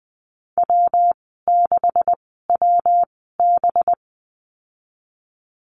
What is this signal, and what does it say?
Morse code, heard as a keyed tone, about 20 wpm, spaced as for 15 wpm, 700 Hz. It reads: W6WB
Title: 193 Lesson 31g - Letter v - Call Signs 20wpm